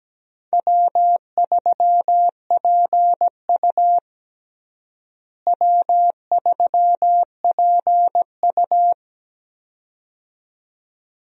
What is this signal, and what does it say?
Morse code, heard as a keyed tone, about 17 wpm, 700 Hz. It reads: W3PU W3PU
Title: USA Call Signs - 4 Characters - Repeated 2x - 17wpm